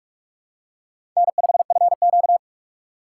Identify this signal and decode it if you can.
Morse code, heard as a keyed tone, about 44 wpm, 690 Hz. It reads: N5FQ
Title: USA Call Signs - 4 Characters - 3x Character Spacing - 45wpm